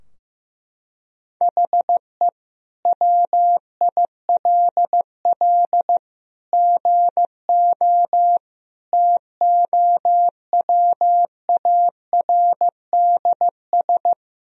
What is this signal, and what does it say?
Morse code, heard as a keyed tone, about 15 wpm, 700 Hz. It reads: HE WILL GO TOWARDS
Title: Sentences from Top 1000 Words - Review Part 2 - 15wpm